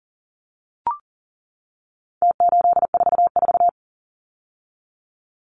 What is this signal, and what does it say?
Morse code, heard as a keyed tone, about 40 wpm, 700 Hz. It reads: T844
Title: Numbers - Contest Style - 4-Digit 40wpm